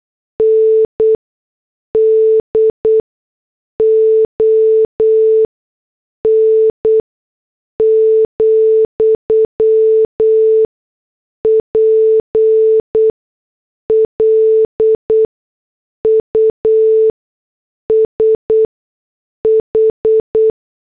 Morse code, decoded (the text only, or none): NDON, PLUS H